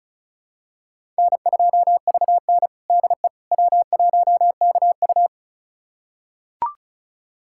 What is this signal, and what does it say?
Morse code, heard as a keyed tone, about 36 wpm, 700 Hz. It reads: N2VN DE W1KU
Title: USA Callsign DE Callsign 35wpm